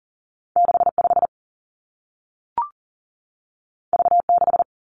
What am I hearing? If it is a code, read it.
Morse code, heard as a keyed tone, about 42 wpm, 700 Hz. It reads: 6 5 V 6
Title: Two Challenging Letter-Number Combinations 40wpm